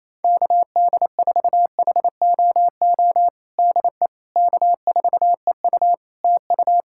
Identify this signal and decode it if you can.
Morse code, heard as a keyed tone, about 27 wpm, 700 Hz. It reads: KD4HOO DE K4EU TU
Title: Practice QSO - Exchange 4 - Ending - 28wpm